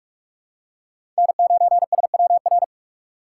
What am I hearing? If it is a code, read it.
Morse code, heard as a keyed tone, about 44 wpm, 700 Hz. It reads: N9SWR
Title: USA Call Signs - 5 Characters - 2x Character Spacing - 45wpm